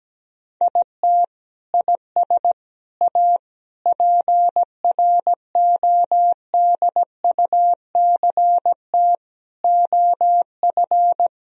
Morse code, heard as a keyed tone, about 17 wpm, 700 Hz. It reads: IT IS A PRODUCT OF
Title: Sentences from Top 1000 Words - Review Part 4 - 17wpm